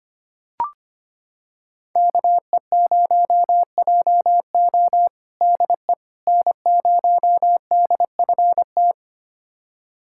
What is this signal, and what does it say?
Morse code, heard as a keyed tone, about 25 wpm, 700 Hz. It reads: KE0JO DE N0DFT
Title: USA Callsign DE Callsign - 5 Characters - 25wpm